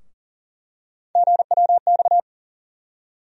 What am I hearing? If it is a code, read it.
Morse code, heard as a keyed tone, about 40 wpm, 700 Hz. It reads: GWX
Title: USA Call Sign Postfixes - 3 Letters 40wpm